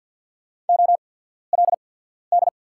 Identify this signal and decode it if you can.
Morse code, heard as a keyed tone, about 48 wpm, spaced as for 14 wpm, 690 Hz. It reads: OLD
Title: 8x Character Spacing - Top 300 Words - 50wpm